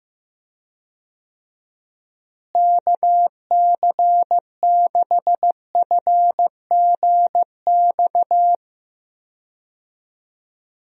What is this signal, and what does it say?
Morse code, heard as a keyed tone, about 15 wpm, 700 Hz. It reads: KC6FGX
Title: USA Call Signs - 6 Characters - Repeated 3x - 15wpm